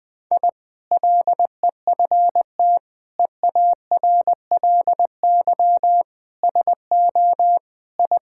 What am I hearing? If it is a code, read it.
Morse code, heard as a keyed tone, about 20 wpm, 700 Hz. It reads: I LEFT EARLY SO I
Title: Sentences with 2-Word Phrases - Spoken Beginning - 20wpm